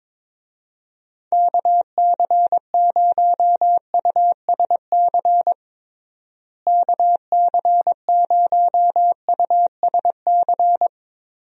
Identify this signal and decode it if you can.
Morse code, heard as a keyed tone, about 22 wpm, 700 Hz. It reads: KC0USC KC0USC
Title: USA Call Signs - 6 Characters - Repeated 3x - 22wpm